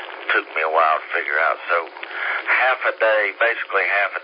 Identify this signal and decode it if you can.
Emergency Alert System (SAME) tones, heard as harsh buzzing bursts, with speech. It is heard from 1.0 to 3.6 s.